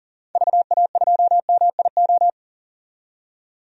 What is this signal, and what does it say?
Morse code, heard as a keyed tone, about 40 wpm, 700 Hz. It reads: VA2MIO
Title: DXCC-100 Call Signs - 6 Characters - 40wpm